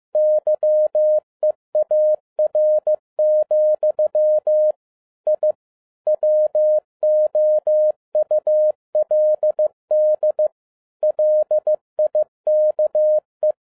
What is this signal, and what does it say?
Morse code, heard as a keyed tone, about 15 wpm, 610 Hz. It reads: YEAR, I WOULD LIKE
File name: 2011-07-mp3 / n12 T8XO 15wpm